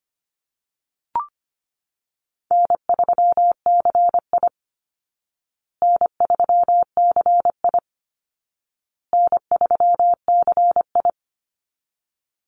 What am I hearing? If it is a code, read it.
Morse code, heard as a keyed tone, about 25 wpm, 700 Hz. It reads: N3CI N3CI N3CI
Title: Call Signs Repeated 2x - 25wpm